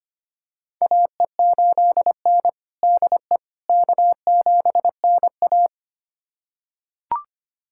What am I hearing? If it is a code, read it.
Morse code, heard as a keyed tone, about 25 wpm, 700 Hz. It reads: AE8N DE K7NA
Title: USA Callsign DE Callsign - 4 Characters - 25wpm